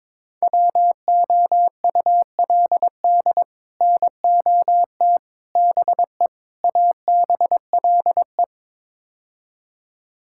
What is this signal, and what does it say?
Morse code, heard as a keyed tone, about 22 wpm, 700 Hz. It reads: WOULD NOT BE ABLE
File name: practice / 22wpm / Build A Phrase - 5-Word Phrases - 22wpm